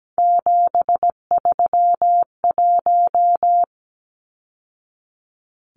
Morse code, heard as a keyed tone, about 17 wpm, 700 Hz. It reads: 731
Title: Numbers - 3-Digit 17wpm